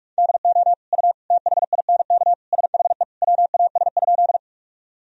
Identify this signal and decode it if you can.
Morse code, heard as a keyed tone, about 46 wpm, 700 Hz. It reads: DO U THINK SHE WAS?
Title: Sentences from Top 200 Words and Abbreviations 45wpm